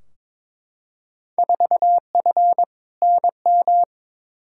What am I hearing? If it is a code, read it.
Morse code, heard as a keyed tone, about 22 wpm, 700 Hz. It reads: 4F NM
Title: Field Day Exchanges Encore - 22wpm